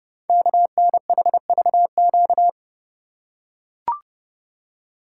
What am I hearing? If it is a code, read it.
Morse code, heard as a keyed tone, about 30 wpm, 690 Hz. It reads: KNHVQ
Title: Code Groups - 5 Letters - 30wpm